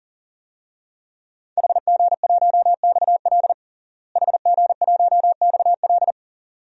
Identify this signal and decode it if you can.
Morse code, heard as a keyed tone, about 40 wpm, 700 Hz. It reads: HG1XL HG1XL
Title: DXCC-100 Call Signs - 5 Characters - Repeated 2x - 40wpm